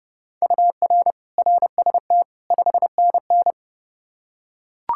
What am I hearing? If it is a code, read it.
Morse code, heard as a keyed tone, about 30 wpm, 700 Hz. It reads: UR RST 5NN
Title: Practice QSO - Exchange 1 - Beginning - 30wpm